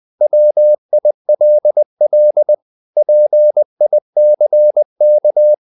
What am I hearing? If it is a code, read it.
Morse code, heard as a keyed tone, about 20 wpm, 600 Hz. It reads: WILL PICK
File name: the old man and the sea / the old man and the sea part 1 20wpm